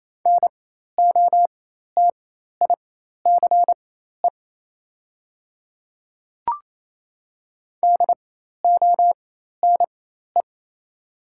Morse code, heard as a keyed tone, about 27 wpm, spaced as for 13 wpm, 700 Hz. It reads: NOTICE DONE
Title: 4x Character Spacing - Top 500 Words - 28wpm